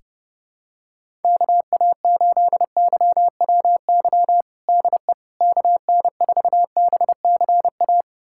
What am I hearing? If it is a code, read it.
Morse code, heard as a keyed tone, about 30 wpm, 700 Hz. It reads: KA8YWY DE KN4BCA
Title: USA Callsign DE Callsign - 6 Characters - 30wpm